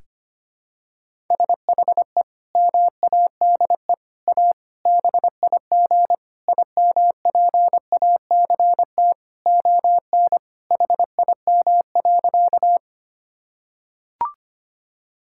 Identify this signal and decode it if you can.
Morse code, heard as a keyed tone, about 25 wpm, 700 Hz. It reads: SHE MADE A BIG IMPACT ON HIM.
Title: Sentences from Top 1000 Words - Review Part 6 - 25wpm